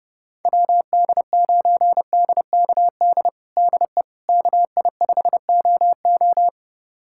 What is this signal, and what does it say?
Morse code, heard as a keyed tone, about 30 wpm, 700 Hz. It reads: WD9DKD DE KI5OO
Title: USA Callsign DE Callsign 30wpm